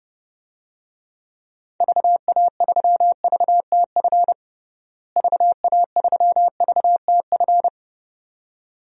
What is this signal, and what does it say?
Morse code, heard as a keyed tone, about 30 wpm, 700 Hz. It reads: VA3VTF VA3VTF
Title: DXCC-100 Call Signs - 6 Characters - Repeated 2x - 30wpm